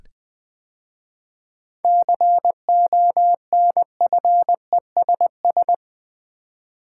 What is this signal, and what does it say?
Morse code, heard as a keyed tone, about 20 wpm, 700 Hz. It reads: CONFESS
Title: 7 Letter Words - First 3 Letters Given - 20wpm